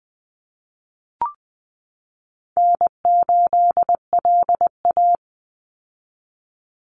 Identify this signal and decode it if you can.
Morse code, heard as a keyed tone, about 20 wpm, 700 Hz. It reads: N8LA
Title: Easy Call Signs Repeated 2x - 20wpm